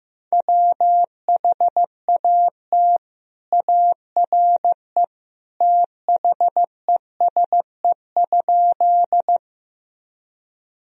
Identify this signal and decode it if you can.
Morse code, heard as a keyed tone, about 15 wpm, 700 Hz. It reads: WHAT ARE THESE?
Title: Sentences from Top 1000 Words - Review Part 1 - 15wpm